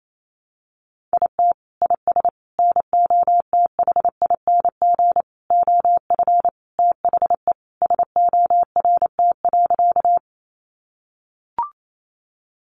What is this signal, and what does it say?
Morse code, heard as a keyed tone, about 28 wpm, 700 Hz. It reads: IT IS NOTHING OF THE SORT.
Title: Sentences from Top 1300 Words 28wpm